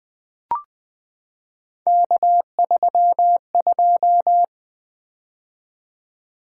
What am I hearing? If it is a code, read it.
Morse code, heard as a keyed tone, about 20 wpm, 700 Hz. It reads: K32
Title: Three Letter-Number Combinations - 20wpm